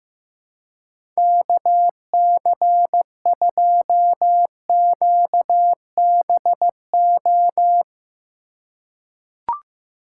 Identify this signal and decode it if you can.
Morse code, heard as a keyed tone, about 15 wpm, 700 Hz. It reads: KC2QBO
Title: USA Call Signs - 6 Characters - 4x Character Spacing - 15wpm